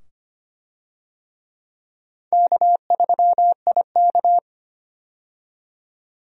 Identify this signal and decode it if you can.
Morse code, heard as a keyed tone, about 25 wpm, 700 Hz. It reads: K3IK
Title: USA Call Signs - 4 Characters - 25wpm